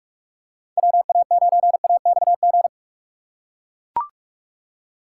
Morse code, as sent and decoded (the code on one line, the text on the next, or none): .-- .- ----. .- -..- --.
WA9AXG